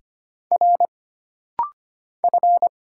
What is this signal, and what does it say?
Morse code, heard as a keyed tone, about 25 wpm, 700 Hz. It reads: R F
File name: Single Character (Letter:Number:Punctuation:Prosign) - Rapid-Fire - 25wpm